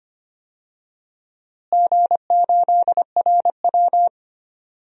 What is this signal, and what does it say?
Morse code, heard as a keyed tone, about 25 wpm, 700 Hz. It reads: G8RW
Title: DXCC-100 Call Signs - 4 Characters - Repeated 3x - 25wpm